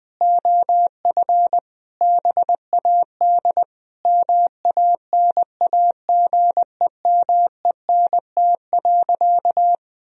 Morse code, heard as a keyed tone, about 21 wpm, 700 Hz. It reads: OF BAD MANAGEMENT.
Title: Sentences from Top 1000 Words - Review Part 4 - 20wpm